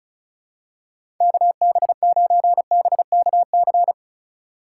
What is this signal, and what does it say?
Morse code, heard as a keyed tone, about 34 wpm, 700 Hz. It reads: KB9BKC
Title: USA Call Signs - 6 Characters - 3x Character Spacing - 35wpm